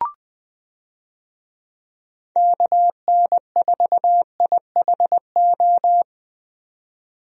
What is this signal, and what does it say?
Morse code, heard as a keyed tone, about 20 wpm, 700 Hz. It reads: KN4IHO
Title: USA Call Signs - 6 Characters - 20wpm